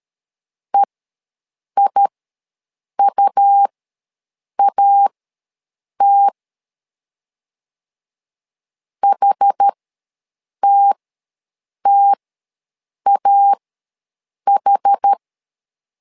Morse code, heard as a keyed tone, about 13 wpm, spaced as for 7 wpm, 800 Hz. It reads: EIUAT HTTAH
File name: LEZIONI / 2 UHDN / 2 UH ws 12wpm 1730